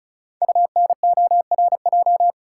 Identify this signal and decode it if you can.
Morse code, heard as a keyed tone, about 36 wpm, 690 Hz. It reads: UNORJ
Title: Code Groups - 5 Letters - 35wpm